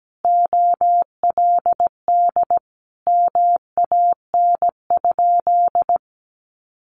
Morse code, heard as a keyed tone, about 17 wpm, 700 Hz. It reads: OLD MAN?
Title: Sentences from Top 1100 Words 17wpm